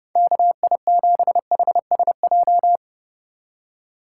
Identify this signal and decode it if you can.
Morse code, heard as a keyed tone, about 32 wpm, 700 Hz. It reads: KI7HSJ
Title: USA Call Signs - 6 Characters - 4x Character Spacing - 30wpm